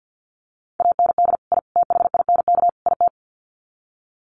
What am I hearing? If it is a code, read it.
Morse code, heard as a keyed tone, about 50 wpm, 700 Hz. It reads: AND I THINK IT
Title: Sets of 4 Words 50wpm